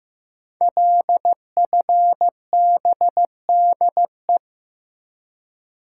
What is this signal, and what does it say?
Morse code, heard as a keyed tone, about 15 wpm, 700 Hz. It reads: LFBDE
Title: Code Groups - 5 Letters - 15wpm